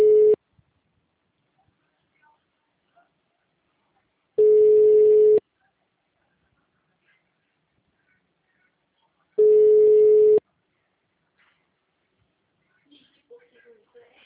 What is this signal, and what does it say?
Ringback tone (Europe, 425 Hz), with speech.
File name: out- -2019- - - .55322